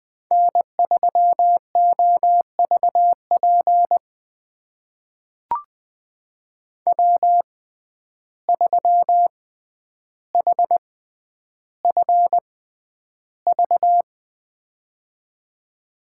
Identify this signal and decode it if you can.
Morse code, heard as a keyed tone, about 20 wpm, spaced as for 10 wpm, 700 Hz. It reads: N3OVP W3HFV
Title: USA Call Signs - 6x Character Spacing - 20wpm